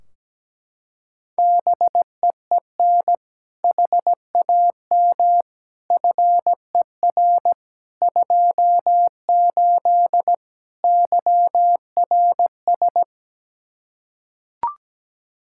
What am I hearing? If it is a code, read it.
Morse code, heard as a keyed tone, about 17 wpm, 700 Hz. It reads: BEEN HAM FER 28 YRS
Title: Practice QSO - Exchange 3 - More Info - 17wpm